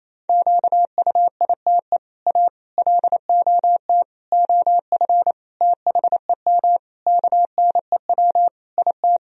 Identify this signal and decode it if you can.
Morse code, heard as a keyed tone, about 28 wpm, 700 Hz. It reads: QUITE A LOT OF THEM KNEW IT
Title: Sentences from Top 1000 Words - Review Part 3 - 28wpm